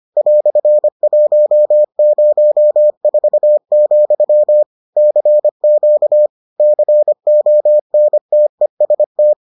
Morse code, heard as a keyed tone, about 25 wpm, 600 Hz. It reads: "104, CQ CONTEST